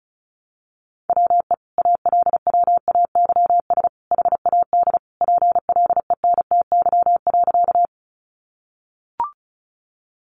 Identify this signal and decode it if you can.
Morse code, heard as a keyed tone, about 34 wpm, 700 Hz. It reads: WE ALWAYS HAD PLENTY.
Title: Sentences from Top 1300 Words 35wpm